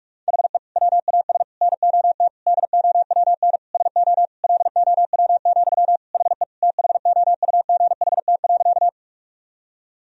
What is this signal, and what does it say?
Morse code, heard as a keyed tone, about 41 wpm, 700 Hz. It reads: HE WAS NOT DOWN SO LOW, HE THOUGHT.
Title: Sentences from Top 1000 Words - Review Part 1 - 45wpm